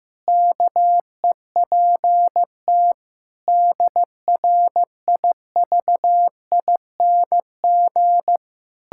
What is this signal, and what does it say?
Morse code, heard as a keyed tone, about 16 wpm, 700 Hz. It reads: KEPT DRIVING
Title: Pangram Challenge v2 - 15wpm